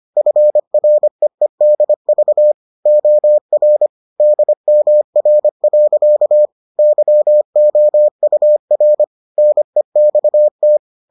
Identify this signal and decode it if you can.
Morse code, heard as a keyed tone, about 25 wpm, 600 Hz. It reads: FREEDV OR DMR. YOUR NEXT